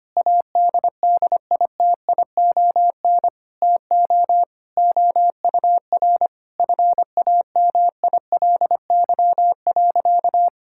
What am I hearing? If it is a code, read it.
Morse code, heard as a keyed tone, about 25 wpm, 700 Hz. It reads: ADDITION TO OUR FAMILY.
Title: Sentences from Top 1000 Words - Review Part 6 - 25wpm